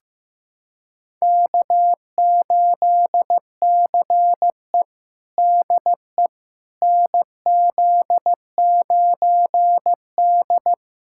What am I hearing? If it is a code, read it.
Morse code, heard as a keyed tone, about 15 wpm, 700 Hz. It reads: K8CE DE NZ9D
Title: USA Callsign DE Callsign - 4 Characters - 15wpm